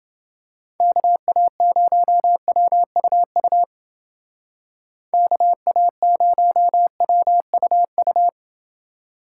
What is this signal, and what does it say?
Morse code, heard as a keyed tone, about 30 wpm, 700 Hz. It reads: KA0WUU KA0WUU